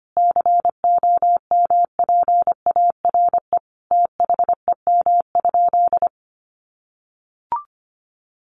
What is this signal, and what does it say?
Morse code, heard as a keyed tone, about 25 wpm, 700 Hz. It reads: COMPARE THEM?
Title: Sentences from Top 1300 Words 25wpm